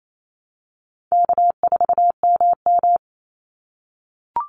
Morse code, heard as a keyed tone, about 28 wpm, 700 Hz. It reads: K4MM
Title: Easy Call Signs - 28wpm